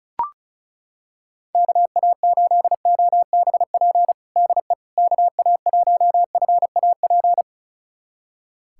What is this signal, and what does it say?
Morse code, heard as a keyed tone, about 36 wpm, 700 Hz. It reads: KA8OBP DE KA1FAP
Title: USA Callsign DE Callsign - 6 Characters - 35wpm